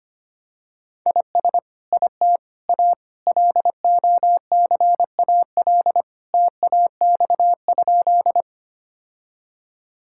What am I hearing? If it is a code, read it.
Morse code, heard as a keyed tone, about 25 wpm, 700 Hz. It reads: IS IT A LOCAL TAX?
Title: Sentences from Top 1000 Words - Review Part 5 - 25wpm